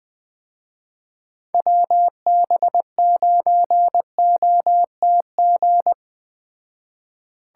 Morse code, heard as a keyed tone, about 21 wpm, 700 Hz. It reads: WB9OTG
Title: Speed Racing - USA 2x3 Call Signs 17wpm